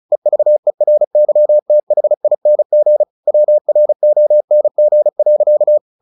Morse code, heard as a keyed tone, about 34 wpm, 600 Hz. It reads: EVERYTHING WRONG.
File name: the old man and the sea 2 35wpm